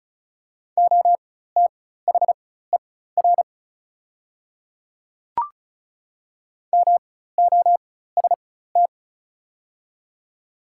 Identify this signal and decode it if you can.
Morse code, heard as a keyed tone, about 36 wpm, spaced as for 16 wpm, 700 Hz. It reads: OTHER MOST
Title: 4x Character Spacing - Top 100 Words - 35wpm